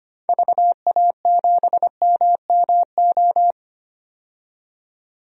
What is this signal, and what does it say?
Morse code, heard as a keyed tone, about 25 wpm, 700 Hz. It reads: VA7MMO
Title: DXCC-100 Call Signs - 6 Characters - 25wpm